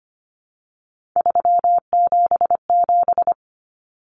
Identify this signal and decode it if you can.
Morse code, heard as a keyed tone, about 25 wpm, 700 Hz. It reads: 377
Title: Numbers - 3-Digit 25wpm